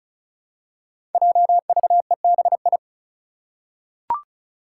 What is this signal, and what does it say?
Morse code, heard as a keyed tone, about 36 wpm, 700 Hz. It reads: JVEBI
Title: Five Challenging Letter-Number Combinations 35wpm